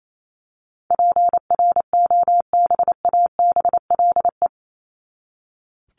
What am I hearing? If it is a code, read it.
Morse code, heard as a keyed tone, about 28 wpm, 700 Hz. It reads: PROBABLE
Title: LY Suffix - Top 100 Words with Base Words 28wpm